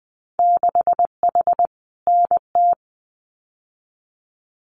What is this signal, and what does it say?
Morse code, heard as a keyed tone, about 20 wpm, 700 Hz. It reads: 6H NT
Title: Winter Field Day Exchanges Encore 20wpm